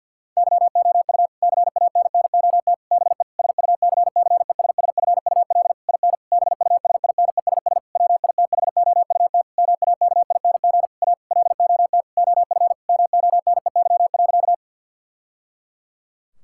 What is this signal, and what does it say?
Morse code, heard as a keyed tone, about 51 wpm, 700 Hz. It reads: YOU CANNOT BE SUCCESSFUL IN BUSINESS WITHOUT MAKING A LOT OF MONEY.
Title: Sentences from Top 1000 Words - Review Part 6 - 50wpm